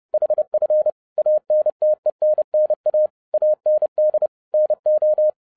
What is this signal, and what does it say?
Morse code, heard as a keyed tone, about 30 wpm, 610 Hz. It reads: HF ANTENNA AND NO
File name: 2012-01-mp3 / n11-I DIDNT MISS THE SOLAR CYCLE 24-30wpm